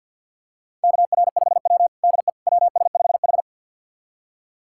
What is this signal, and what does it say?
Morse code, heard as a keyed tone, about 50 wpm, 700 Hz. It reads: KR5W DE WS5H
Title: USA Callsign DE Callsign - 4 Characters - 50wpm